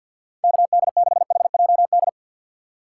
Morse code, heard as a keyed tone, about 48 wpm, 700 Hz. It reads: KD6HJD
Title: USA Call Signs - 6 Characters - 6x Character Spacing - 50wpm